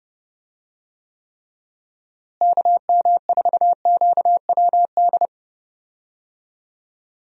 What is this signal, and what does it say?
Morse code, heard as a keyed tone, about 30 wpm, 700 Hz. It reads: KM4QWD
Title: USA Call Signs - 6 Characters - Repeated 3x - 30wpm